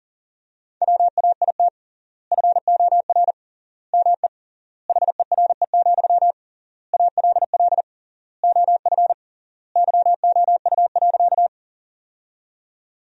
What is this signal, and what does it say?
Morse code, heard as a keyed tone, about 40 wpm, 700 Hz. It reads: WAIT FOR ME HERE, ALL OF YOU.
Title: 3x Word-Spacing - Sentences from Top 500 Words - 40wpm